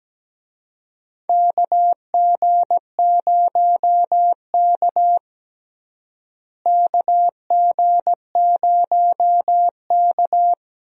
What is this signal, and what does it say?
Morse code, heard as a keyed tone, about 17 wpm, 700 Hz. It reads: KG0K KG0K
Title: USA Call Signs - 4 Characters - Repeated 2x - 17wpm